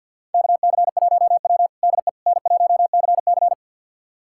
Morse code, heard as a keyed tone, about 50 wpm, 700 Hz. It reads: KC1W DE N1CC
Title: USA Callsign DE Callsign - 4 Characters - 50wpm